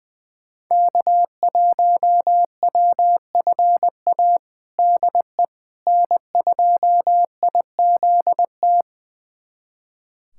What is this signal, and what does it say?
Morse code, heard as a keyed tone, about 20 wpm, 700 Hz. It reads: K1WFA DE N2IZT
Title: USA Callsign DE Callsign - 5 Characters - 20wpm